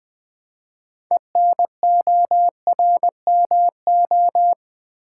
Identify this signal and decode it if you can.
Morse code, heard as a keyed tone, about 20 wpm, 700 Hz. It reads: ENORMO
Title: Build A Word - 2 Letters At A Time - 8-Letter Words - 20wpm